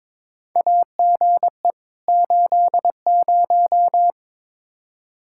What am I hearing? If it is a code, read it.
Morse code, heard as a keyed tone, about 22 wpm, 700 Hz. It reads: AGE 80
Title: K1USN SST QSO Practice 22wpm